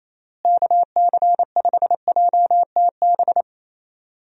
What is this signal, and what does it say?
Morse code, heard as a keyed tone, about 28 wpm, 700 Hz. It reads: KC5JTB
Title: USA Call Signs - 6 Characters - Repeated 3x - 28wpm